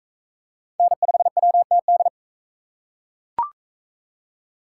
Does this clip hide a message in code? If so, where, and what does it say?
Yes: Morse code from 0.8 to 2.1 s, heard as a keyed tone, about 38 wpm, 700 Hz. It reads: N5WTD